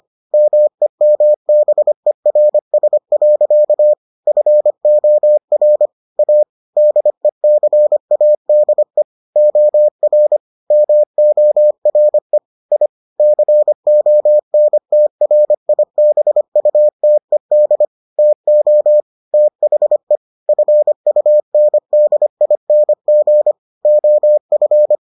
Morse code, heard as a keyed tone, about 25 wpm, 600 Hz. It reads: MEMBERS. FOR A DECADE OR MORE I CONTRIBUTED TO THE FUNDING OF